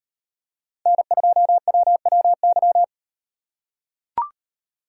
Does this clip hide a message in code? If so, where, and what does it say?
Yes: Morse code from 0.9 to 2.8 s, heard as a keyed tone, about 37 wpm, 700 Hz. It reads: N2WWY